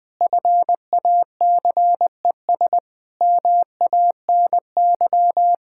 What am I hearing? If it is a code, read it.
Morse code, heard as a keyed tone, about 20 wpm, 700 Hz. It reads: FACES MANY
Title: Sentences with 7-Word Phrases - 20wpm